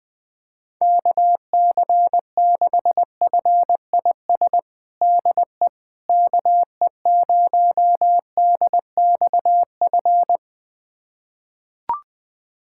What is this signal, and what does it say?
Morse code, heard as a keyed tone, about 20 wpm, 700 Hz. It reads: KC6FIS DE KE0DXF
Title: USA Callsign DE Callsign - 6 Characters - 20wpm